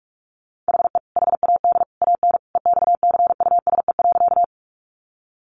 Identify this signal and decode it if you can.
Morse code, heard as a keyed tone, about 44 wpm, 700 Hz. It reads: HE HAD AN EXCUSE.
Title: Sentences from Top 800 Words 45wpm